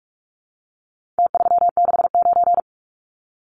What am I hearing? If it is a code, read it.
Morse code, heard as a keyed tone, about 44 wpm, 700 Hz. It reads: T369
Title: Numbers - Contest Style - 4-Digit 45wpm